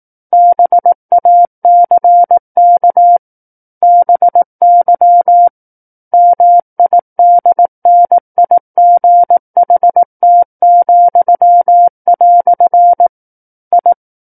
Morse code, heard as a keyed tone, about 18 wpm, 700 Hz. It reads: BACK BY MIDNIGHT," I
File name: 18 WPM